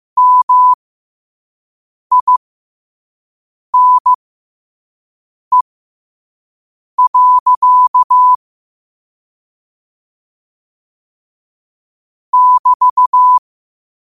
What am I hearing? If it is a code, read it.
Morse code, heard as a keyed tone, about 15 wpm, spaced as for 5 wpm, 1000 Hz. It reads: MINE. <BT>
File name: qotdfiles / 5wpm